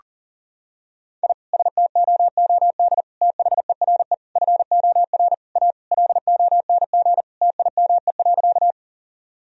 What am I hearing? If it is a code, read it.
Morse code, heard as a keyed tone, about 40 wpm, 700 Hz. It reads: I STOOD THERE FOR A LONG TIME.
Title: Sentences from Top 1000 Words - Review Part 2 - 40wpm